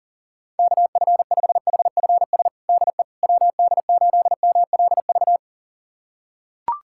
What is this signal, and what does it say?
Morse code, heard as a keyed tone, about 40 wpm, 700 Hz. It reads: KF5HFS DE WD8MLV
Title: USA Callsign DE Callsign - 6 Characters - 40wpm